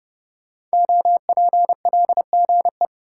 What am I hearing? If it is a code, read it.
Morse code, heard as a keyed tone, about 30 wpm, 700 Hz. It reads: OPLGE
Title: Code Groups - 5 Letters - 30wpm